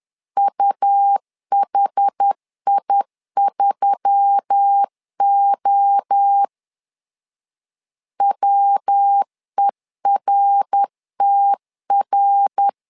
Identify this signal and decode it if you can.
Morse code, heard as a keyed tone, about 10 wpm, 800 Hz. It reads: UHI3O WERTR